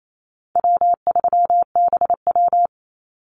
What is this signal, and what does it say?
Morse code, heard as a keyed tone, about 28 wpm, 700 Hz. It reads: W3BW
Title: Easy Call Signs - 28wpm